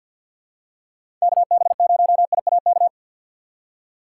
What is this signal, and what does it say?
Morse code, heard as a keyed tone, about 50 wpm, 700 Hz. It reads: KB0IAK
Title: USA Call Signs - 6 Characters - 6x Character Spacing - 50wpm